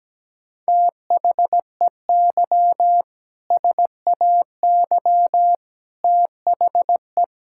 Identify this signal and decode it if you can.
Morse code, heard as a keyed tone, about 17 wpm, 700 Hz. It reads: THEY SAY THE
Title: Sentences from Top 1600 Words 17wpm